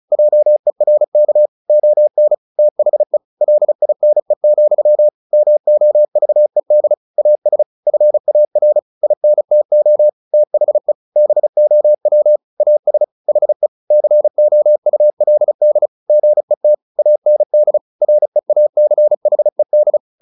Morse code, heard as a keyed tone, about 34 wpm, 600 Hz. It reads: JERK ON THE LINE, MOVED AS FAR INTO THE BOW AS HE COULD GET AND REACHED
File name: the old man and the sea / the old man and the sea 1 35wpm